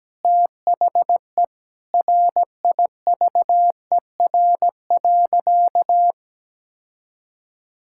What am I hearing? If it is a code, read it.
Morse code, heard as a keyed tone, about 17 wpm, 700 Hz. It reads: THE RIVER.
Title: Sentences from Top 1000 Words - Review Part 5 - 17wpm